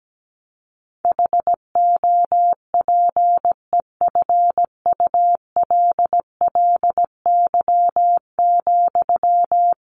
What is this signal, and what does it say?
Morse code, heard as a keyed tone, about 17 wpm, 700 Hz. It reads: HOPEFULLY,
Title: Sentences from Top 1500 Words 17wpm